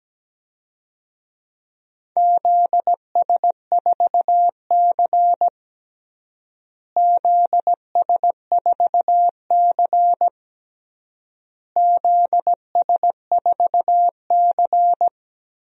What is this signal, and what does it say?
Morse code, heard as a keyed tone, about 17 wpm, 700 Hz. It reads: ZS4C ZS4C ZS4C
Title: DXCC-100 Call Signs - 4 Characters - Repeated 3x - 17wpm